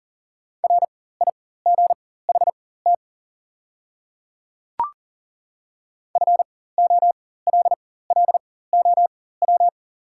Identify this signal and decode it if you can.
Morse code, heard as a keyed tone, about 40 wpm, spaced as for 19 wpm, 700 Hz. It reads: RIGHT FOLLOW
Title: 4x Character Spacing - Top 200 Words - 40wpm